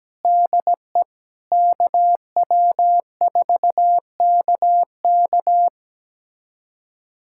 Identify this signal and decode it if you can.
Morse code, heard as a keyed tone, about 17 wpm, 700 Hz. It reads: DE KW4KK
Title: USA Callsign DE Callsign 17wpm